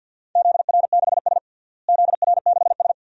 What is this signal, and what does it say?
Morse code, heard as a keyed tone, about 52 wpm, 700 Hz. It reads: ZR6S ZR6S
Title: DXCC-100 Call Signs - 4 Characters - Repeated 2x - 50wpm